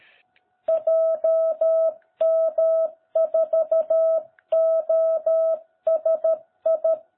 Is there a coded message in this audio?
Yes: Morse code from 0.7 to 6.9 s, heard as a keyed tone, about 13 wpm, 660 Hz. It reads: JM4OSI